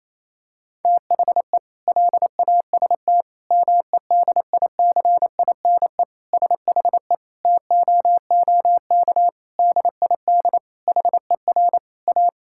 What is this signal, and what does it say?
Morse code, heard as a keyed tone, about 28 wpm, 700 Hz. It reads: THE LAST MEDICINE SHE TOOK DID HER A
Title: Sentences from Top 1000 Words - Review Part 6 - 28wpm